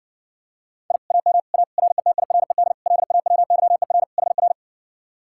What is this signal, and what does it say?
Morse code, heard as a keyed tone, about 60 wpm, 700 Hz. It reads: I AM A RETIRED LAWYER HR